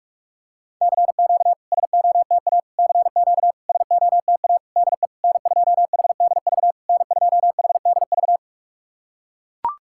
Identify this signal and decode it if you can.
Morse code, heard as a keyed tone, about 41 wpm, 690 Hz. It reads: CQ SOTA CQ SOTA DE N2HDV N2HDV
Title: SOTA QSO Practice 45wpm